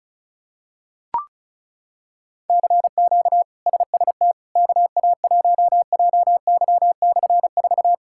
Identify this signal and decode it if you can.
Morse code, heard as a keyed tone, about 34 wpm, 700 Hz. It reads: CQ SST KA1JY/4
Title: K1USN SST QSO Practice - 35wpm